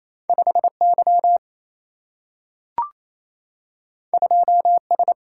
Morse code, heard as a keyed tone, about 28 wpm, 700 Hz. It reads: 5 Y 2 S
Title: Single-Letter Single-Number 28wpm